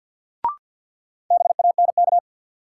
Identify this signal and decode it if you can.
Morse code, heard as a keyed tone, about 54 wpm, 700 Hz. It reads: BANK
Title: Sets of 5 Words - New Word Drill - 50wpm